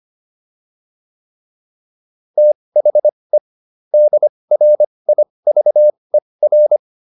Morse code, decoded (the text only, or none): THE DRIVER